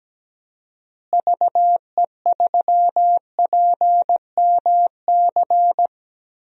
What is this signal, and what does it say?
Morse code, heard as a keyed tone, about 17 wpm, 700 Hz. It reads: VE3PMC
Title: DXCC-100 Call Signs - 6 Characters - Repeated 3x - 17wpm